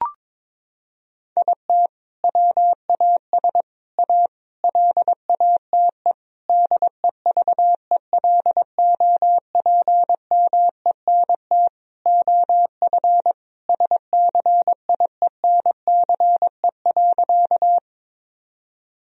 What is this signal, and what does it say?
Morse code, heard as a keyed tone, about 22 wpm, 700 Hz. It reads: IT WAS A LATE DEVELOPMENT OF SCIENCE.
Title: Sentences from Top 1000 Words - Review Part 4 - 22wpm